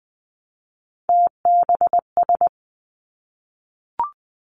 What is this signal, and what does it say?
Morse code, heard as a keyed tone, about 20 wpm, 700 Hz. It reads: TBS
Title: Three Challenging Letter-Number Combinations 20wpm